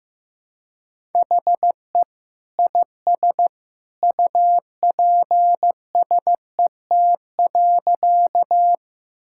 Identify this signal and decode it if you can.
Morse code, heard as a keyed tone, about 14 wpm, 700 Hz. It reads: HE IS UPSET.
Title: Sentences from Top 1000 Words - Review Part 5 - 15wpm